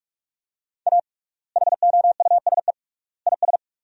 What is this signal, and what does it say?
Morse code, heard as a keyed tone, about 41 wpm, 700 Hz. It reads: A HOUSE IS
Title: 3x Word-Spacing - Sentences from Top 400 Words - 45wpm